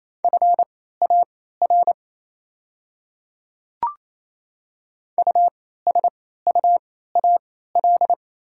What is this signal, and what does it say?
Morse code, heard as a keyed tone, about 27 wpm, spaced as for 16 wpm, 690 Hz. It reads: FAR USUAL
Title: 3x Character Spacing - Top 500 Words - 28wpm